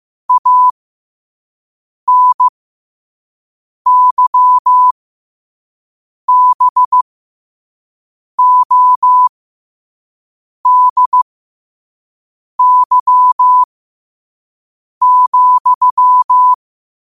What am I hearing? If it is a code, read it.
Morse code, heard as a keyed tone, about 15 wpm, spaced as for 5 wpm, 1000 Hz. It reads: ANYBODY,
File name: 5wpm